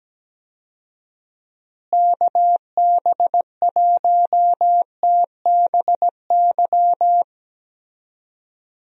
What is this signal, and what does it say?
Morse code, heard as a keyed tone, about 17 wpm, 700 Hz. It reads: KB1TBY
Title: USA Call Signs - 6 Characters - Repeated 3x - 17wpm